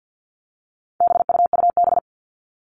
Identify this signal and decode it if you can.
Morse code, heard as a keyed tone, about 50 wpm, 700 Hz. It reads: BUUB
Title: All Cut Numbers - 4-Digit 50wpm